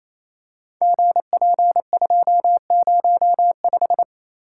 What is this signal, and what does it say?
Morse code, heard as a keyed tone, about 28 wpm, 700 Hz. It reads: GP205
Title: Code Groups - 5 Letter:Numbers - 28wpm